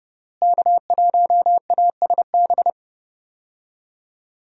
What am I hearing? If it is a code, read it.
Morse code, heard as a keyed tone, about 30 wpm, 700 Hz. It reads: K1ASB
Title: USA Call Signs - 5 Characters - Repeated 3x - 30wpm